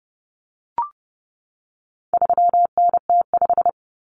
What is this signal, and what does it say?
Morse code, heard as a keyed tone, about 30 wpm, 700 Hz. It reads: 3NT5
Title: Common Cut Numbers - 4-Digit 30wpm